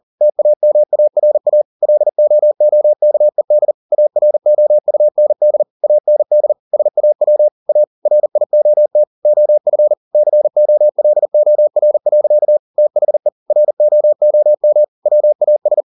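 Morse code, heard as a keyed tone, about 40 wpm, 600 Hz. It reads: TAMARA LOOKED AROUND AND SAW A RIOT OF COLOR. THE ROOM WAS
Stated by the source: Morse code at 40 wpm, practice text, CW med Gnister.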